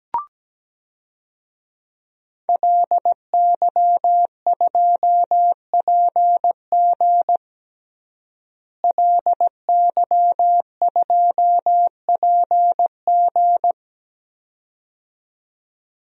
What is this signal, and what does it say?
Morse code, heard as a keyed tone, about 17 wpm, 700 Hz. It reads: LY2PG LY2PG
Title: DXCC-100 Call Signs - 5 Characters - Repeated 2x - 17wpm